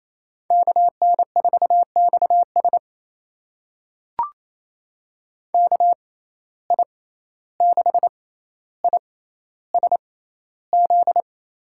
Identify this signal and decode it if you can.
Morse code, heard as a keyed tone, about 28 wpm, spaced as for 14 wpm, 700 Hz. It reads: KN4XS KI6ISZ
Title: USA Call Signs - 6x Character Spacing - 28wpm